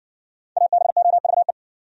Morse code, heard as a keyed tone, about 60 wpm, 700 Hz. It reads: ABOVE